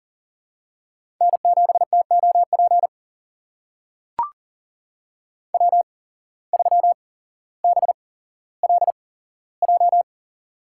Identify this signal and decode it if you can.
Morse code, heard as a keyed tone, about 39 wpm, spaced as for 17 wpm, 700 Hz. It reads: N7TOP W3BLJ
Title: USA Call Signs - 5 Characters - 8x Character Spacing - 40wpm